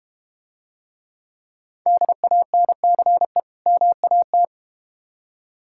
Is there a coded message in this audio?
Yes: Morse code from 1.9 to 4.5 s, heard as a keyed tone, about 32 wpm, 700 Hz. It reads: DANCE MAT